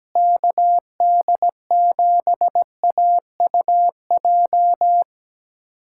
Morse code, heard as a keyed tone, about 17 wpm, 700 Hz. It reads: KD7AUJ
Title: USA Callsign DE Callsign - 6 Characters - 17wpm